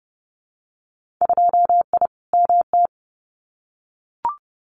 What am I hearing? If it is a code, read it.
Morse code, heard as a keyed tone, about 30 wpm, 700 Hz. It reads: 2I MT
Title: Winter Field Day Exchanges Encore 30wpm